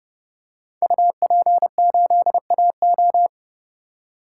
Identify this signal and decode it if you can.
Morse code, heard as a keyed tone, about 30 wpm, 700 Hz. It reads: UP8AO
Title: Code Groups - 5 Letter:Numbers - 30wpm